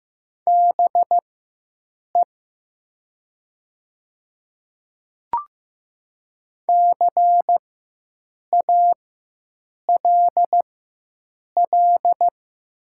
Morse code, heard as a keyed tone, about 15 wpm, spaced as for 7 wpm, 700 Hz. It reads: BE CALL
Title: 4x Character Spacing - Top 100 Words - 15wpm